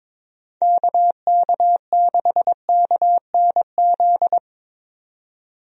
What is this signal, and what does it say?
Morse code, heard as a keyed tone, about 21 wpm, 700 Hz. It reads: KK6KNZ
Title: USA Call Signs - 6 Characters - 22wpm